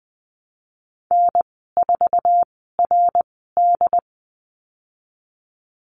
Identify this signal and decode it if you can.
Morse code, heard as a keyed tone, about 20 wpm, spaced as for 15 wpm, 700 Hz. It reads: N4RD
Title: 069 Lesson 12g - Letter d - Call Signs 20wpm